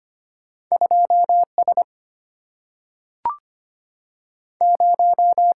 Morse code, heard as a keyed tone, about 25 wpm, 700 Hz. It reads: 2 S 0
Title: Single-Letter Single-Number 25wpm